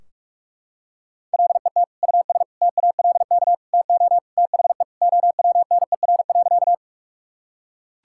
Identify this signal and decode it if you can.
Morse code, heard as a keyed tone, about 46 wpm, 700 Hz. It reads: LET US TALK TO THE OWNER.
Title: Sentences from Top 1600 Words 45wpm